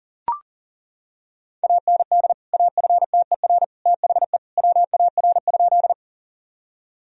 Morse code, heard as a keyed tone, about 40 wpm, 700 Hz. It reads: AND AFTER THE WAR?
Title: Sentences from Top 1000 Words - Review Part 2 - 40wpm